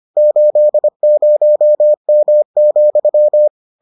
Morse code, heard as a keyed tone, about 25 wpm, 600 Hz. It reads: 80M,